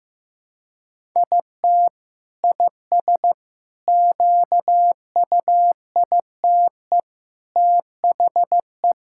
Morse code, heard as a keyed tone, about 15 wpm, 700 Hz. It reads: IT IS QUITE THE
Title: Sentences from Top 1000 Words - Review Part 5 - 15wpm